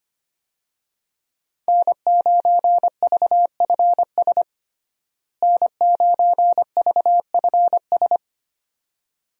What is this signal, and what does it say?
Morse code, heard as a keyed tone, about 25 wpm, 700 Hz. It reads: N9VFS N9VFS
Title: USA Call Signs - 5 Characters - Repeated 2x - 25wpm